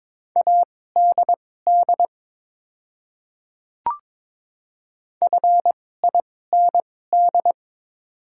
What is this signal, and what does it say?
Morse code, heard as a keyed tone, about 22 wpm, spaced as for 16 wpm, 700 Hz. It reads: ADD FIND
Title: 2x Character Spacing - Top 200 Words - 22wpm